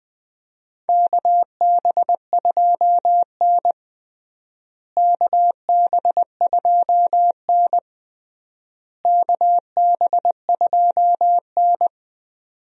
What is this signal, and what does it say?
Morse code, heard as a keyed tone, about 20 wpm, 700 Hz. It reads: KB2N KB2N KB2N
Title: USA Call Signs - 4 Characters - Repeated 3x - 20wpm